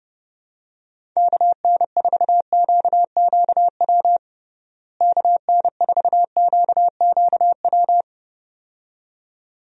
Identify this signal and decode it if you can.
Morse code, heard as a keyed tone, about 29 wpm, 700 Hz. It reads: KN4QQW KN4QQW
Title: USA Call Signs - 6 Characters - Repeated 2x - 30wpm